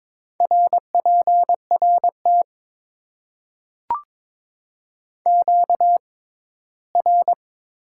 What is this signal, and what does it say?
Morse code, heard as a keyed tone, about 21 wpm, 700 Hz. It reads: R P R T Q R
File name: Top 90 QSO Elements - 6x Character Spacing - 22wpm